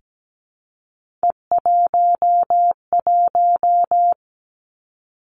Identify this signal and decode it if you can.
Morse code, heard as a keyed tone, about 17 wpm, 700 Hz. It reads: E11
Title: Three Challenging Letter-Number Combinations 17wpm